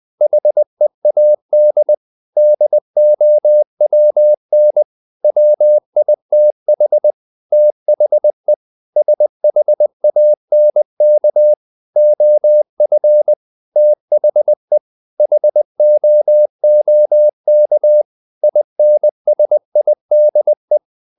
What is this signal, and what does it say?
Morse code, heard as a keyed tone, about 20 wpm, 600 Hz. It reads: HEAD DOWN WITH THE SHANK OF THE HOOK INSIDE
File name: the old man and the sea / the old man and the sea part 1 20wpm